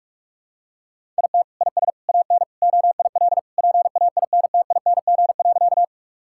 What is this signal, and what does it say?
Morse code, heard as a keyed tone, about 46 wpm, 700 Hz. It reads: IT IS AN OIL PAINTING.
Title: Sentences from Top 1000 Words - Review Part 6 - 45wpm